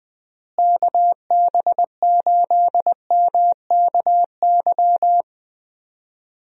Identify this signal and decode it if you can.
Morse code, heard as a keyed tone, about 21 wpm, 700 Hz. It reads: KB8MKY
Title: USA Call Signs - 6 Characters - Repeated 3x - 20wpm